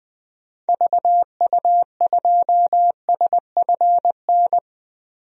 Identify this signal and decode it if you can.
Morse code, heard as a keyed tone, about 20 wpm, 700 Hz. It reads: VU2SFN
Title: DXCC-100 Call Signs - 6 Characters - 20wpm